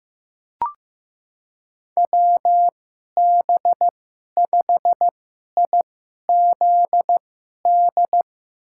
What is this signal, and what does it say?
Morse code, heard as a keyed tone, about 15 wpm, spaced as for 11 wpm, 700 Hz. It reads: WB5IZD
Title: USA Call Signs - 6 Characters - 2x Character Spacing - 15wpm